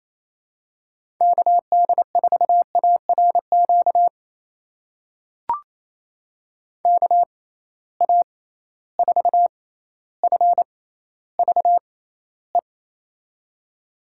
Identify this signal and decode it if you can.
Morse code, heard as a keyed tone, about 28 wpm, spaced as for 14 wpm, 700 Hz. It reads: KD4ARQ KA4FVE
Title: USA Call Signs - 6 Characters - 6x Character Spacing - 28wpm